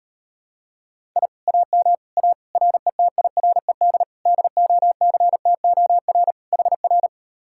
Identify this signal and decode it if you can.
Morse code, heard as a keyed tone, about 38 wpm, 700 Hz. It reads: I AM A RETIRED DOCTOR HR